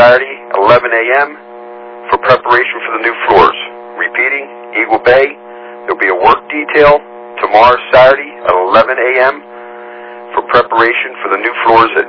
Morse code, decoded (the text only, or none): none